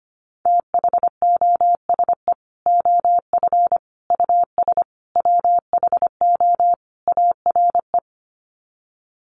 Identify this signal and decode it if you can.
Morse code, heard as a keyed tone, about 25 wpm, 700 Hz. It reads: THOSE OF US WHO ARE
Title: Sets of 5 Words Encore2 25wpm